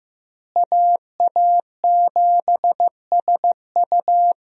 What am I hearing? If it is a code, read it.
Morse code, heard as a keyed tone, about 15 wpm, 700 Hz. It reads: AA7SU
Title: USA Call Signs - 5 Characters - Repeated 3x - 15wpm